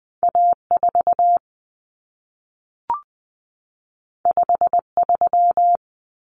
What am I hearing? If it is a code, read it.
Morse code, heard as a keyed tone, about 20 wpm, 700 Hz. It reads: A4 53
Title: Common Cut Numbers - 2-Digit 20wpm